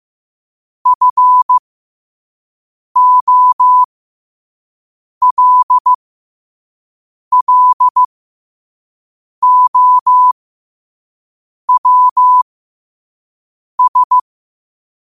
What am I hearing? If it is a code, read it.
Morse code, heard as a keyed tone, about 15 wpm, spaced as for 5 wpm, 1000 Hz. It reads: FOLLOWS
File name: qotdfiles / 5wpm